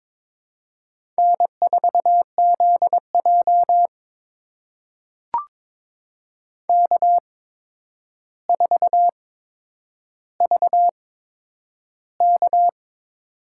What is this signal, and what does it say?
Morse code, heard as a keyed tone, about 23 wpm, spaced as for 10 wpm, 700 Hz. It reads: N4ZJ K4VK
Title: USA Call Signs - 4 Characters - 8x Character Spacing - 22wpm